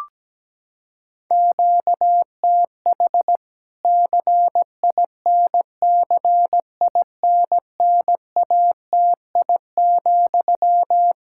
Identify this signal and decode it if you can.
Morse code, heard as a keyed tone, about 17 wpm, 700 Hz. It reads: QTH CINCINNATI,
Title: K1USN SST QSO Practice 17wpm